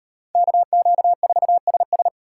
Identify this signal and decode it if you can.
Morse code, heard as a keyed tone, about 38 wpm, 700 Hz. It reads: KQ4SS